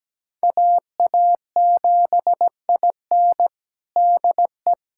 Morse code, heard as a keyed tone, about 17 wpm, 700 Hz. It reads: AA7IN DE
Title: USA Callsign DE Callsign 17wpm